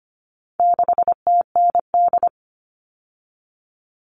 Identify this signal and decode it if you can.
Morse code, heard as a keyed tone, about 25 wpm, 700 Hz. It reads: 6TND
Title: All Cut Numbers - 4-Digit 25wpm